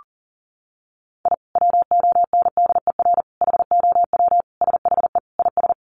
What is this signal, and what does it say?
Morse code, heard as a keyed tone, about 40 wpm, 700 Hz. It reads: I WONDER HOW SHE IS
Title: Sentences from Top 1500 Words 40wpm